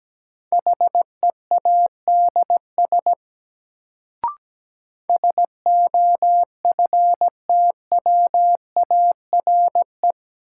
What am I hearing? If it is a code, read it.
Morse code, heard as a keyed tone, about 17 wpm, 700 Hz. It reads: HEADS SOFTWARE
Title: Sentences from Top 1000 Words - Review Part 3 - New Word Drill - 17wpm